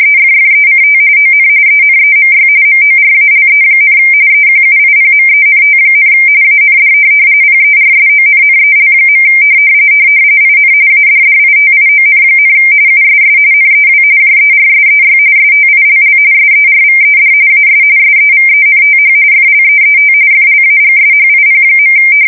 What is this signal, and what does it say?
RTTY, about 45.45 baud. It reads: EFFECT ON ENEMY MORALE, EQUIPMENT, AND PERSONNEL. IN ADDITION, USS NEW JERSEY STILL SERVED AS SEVENTH FLEET FLAGSHIP.
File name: fsk170 / USS-New-Jersey